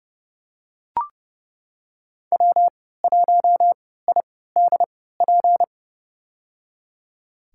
Morse code, heard as a keyed tone, about 30 wpm, spaced as for 17 wpm, 700 Hz. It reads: W1IDP
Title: USA Call Signs - 5 Characters - 3x Character Spacing - 30wpm